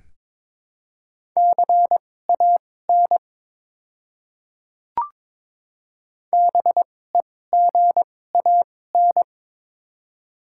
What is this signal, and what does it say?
Morse code, heard as a keyed tone, about 21 wpm, spaced as for 16 wpm, 700 Hz. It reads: CAN BEGAN
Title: 2x Character Spacing - Top 500 Words - 22wpm